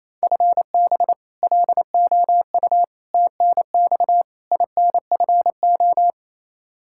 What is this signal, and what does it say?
Morse code, heard as a keyed tone, about 28 wpm, 700 Hz. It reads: FB LOU TNX INFO
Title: K1USN SST QSO Practice 28wpm